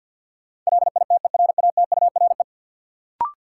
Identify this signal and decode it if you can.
Morse code, heard as a keyed tone, about 52 wpm, 700 Hz. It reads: LITERATURE
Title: Sentences from Top 600 Words - New Word Drill - 50wpm